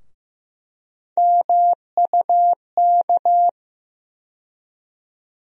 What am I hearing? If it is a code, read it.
Morse code, heard as a keyed tone, about 15 wpm, 700 Hz. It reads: MUK
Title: USA Call Sign Postfixes - 3 Letters 15wpm